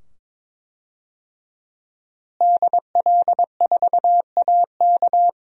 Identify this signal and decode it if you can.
Morse code, heard as a keyed tone, about 22 wpm, 700 Hz. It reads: DL4AK
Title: DXCC-100 Call Signs - 5 Characters - Repeated 2x - 22wpm